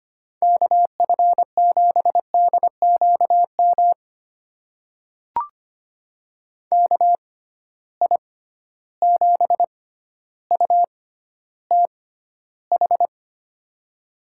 Morse code, heard as a keyed tone, about 25 wpm, spaced as for 13 wpm, 700 Hz. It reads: KF7DQM KI7UTH
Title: USA Call Signs - 6 Characters - 6x Character Spacing - 25wpm